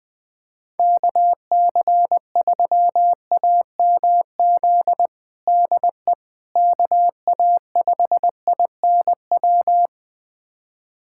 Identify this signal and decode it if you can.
Morse code, heard as a keyed tone, about 21 wpm, 700 Hz. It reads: KC3AMZ DE KA5INW
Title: USA Callsign DE Callsign - 6 Characters - 20wpm